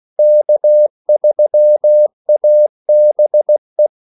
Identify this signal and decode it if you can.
Morse code, heard as a keyed tone, about 16 wpm, 600 Hz. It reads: K3ABE
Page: Kaldesignaler 16 wpm